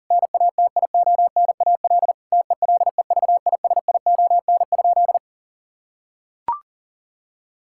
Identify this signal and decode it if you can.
Morse code, heard as a keyed tone, about 37 wpm, 700 Hz. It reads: NATIONAL TELEVISION?
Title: Sentences from Top 1000 Words - Review Part 5 - 40wpm